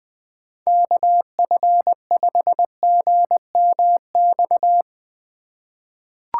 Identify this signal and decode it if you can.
Morse code, heard as a keyed tone, about 20 wpm, 700 Hz. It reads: KF5GMX
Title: USA Call Signs - 6 Characters - 3x Character Spacing - 20wpm